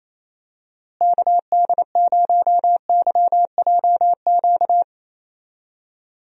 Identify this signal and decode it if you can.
Morse code, heard as a keyed tone, about 28 wpm, 700 Hz. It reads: KD0YJQ
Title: USA Call Signs - 6 Characters - Repeated 3x - 28wpm